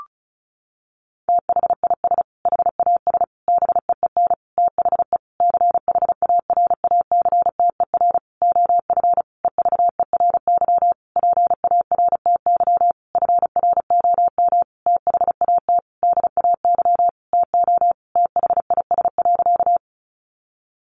Morse code, heard as a keyed tone, about 34 wpm, 700 Hz. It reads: THIS HAS BEEN THE CHARACTER OF EVERY PARTY FROM THAT DAY TO THIS.
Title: Sentences from Top 1400 Words 35wpm